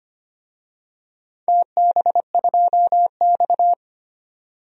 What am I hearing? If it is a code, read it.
Morse code, heard as a keyed tone, about 25 wpm, 700 Hz. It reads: TB2X
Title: DXCC-100 Call Signs - 4 Characters - 25wpm